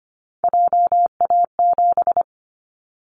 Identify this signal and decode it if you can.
Morse code, heard as a keyed tone, about 25 wpm, 700 Hz. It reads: JA7
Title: Callsign Prefixes for Top 25 DXCC - 25wpm